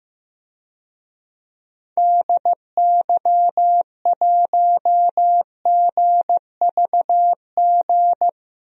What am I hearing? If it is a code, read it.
Morse code, heard as a keyed tone, about 15 wpm, 700 Hz. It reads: DY1GVG
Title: DXCC-100 Call Signs - 6 Characters - Repeated 2x - 15wpm